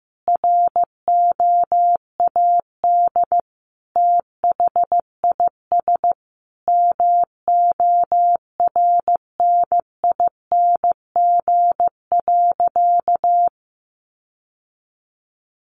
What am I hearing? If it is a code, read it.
Morse code, heard as a keyed tone, about 15 wpm, 700 Hz. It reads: ROAD THIS MORNING.
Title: Sentences from Top 1100 Words 15wpm